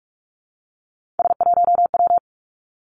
Morse code, heard as a keyed tone, about 46 wpm, 700 Hz. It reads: S1W
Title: Three Challenging Letter-Number Combinations 45wpm